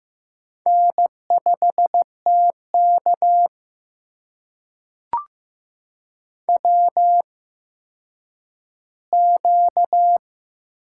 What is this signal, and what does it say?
Morse code, heard as a keyed tone, about 15 wpm, spaced as for 9 wpm, 700 Hz. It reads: N5TK WQ
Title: USA Call Signs - 4 Characters - 8x Character Spacing - 15wpm